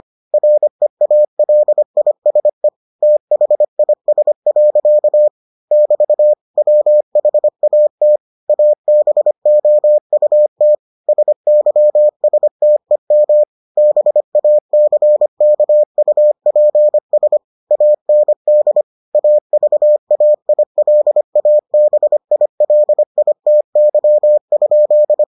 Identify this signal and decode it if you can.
Morse code, heard as a keyed tone, about 25 wpm, 600 Hz. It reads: REALISE THIS. <BT> WHAT ABOUT SYSTEM BACKUPS AND AVAILABILITY?